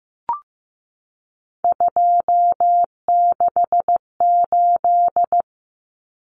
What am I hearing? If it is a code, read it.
Morse code, heard as a keyed tone, about 15 wpm, 700 Hz. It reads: 268
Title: Common Cut Numbers - 3-Digit 15wpm